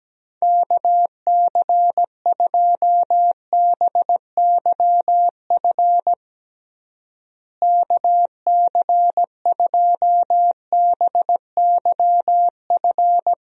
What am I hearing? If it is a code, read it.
Morse code, heard as a keyed tone, about 17 wpm, 700 Hz. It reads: KC2BYF KC2BYF
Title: USA Call Signs - 6 Characters - Repeated 3x - 17wpm